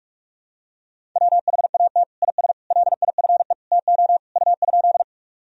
Morse code, heard as a keyed tone, about 42 wpm, 700 Hz. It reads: WHAT IS LIFE TO U?
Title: Sentences from Top 300 Words and Abbreviations 45wpm